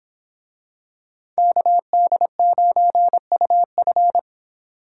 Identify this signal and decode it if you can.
Morse code, heard as a keyed tone, about 24 wpm, 700 Hz. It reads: KD9UF